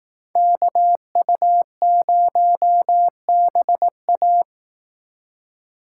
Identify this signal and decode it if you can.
Morse code, heard as a keyed tone, about 18 wpm, 700 Hz. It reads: KU0BA
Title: Speed Racing - USA 2x2 Call Signs 15wpm